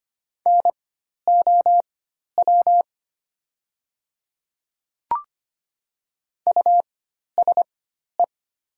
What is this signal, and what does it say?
Morse code, heard as a keyed tone, about 25 wpm, spaced as for 12 wpm, 700 Hz. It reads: NOW USE
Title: 4x Character Spacing - Top 100 Words - 25wpm